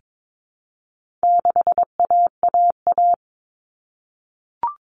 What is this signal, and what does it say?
Morse code, heard as a keyed tone, about 22 wpm, 700 Hz. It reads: 6AAA
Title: Common Cut Numbers - 4-Digit 22wpm